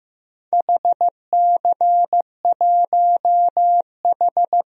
Morse code, heard as a keyed tone, about 15 wpm, 700 Hz. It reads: HC1H
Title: DXCC-100 Call Signs - 4 Characters - Repeated 2x - 15wpm